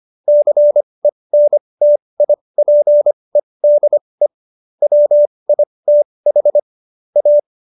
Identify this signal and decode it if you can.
Morse code, heard as a keyed tone, about 25 wpm, spaced as for 20 wpm, 590 Hz. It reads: CENTIPEDE WITH A
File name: Jokes-25-char-20-eff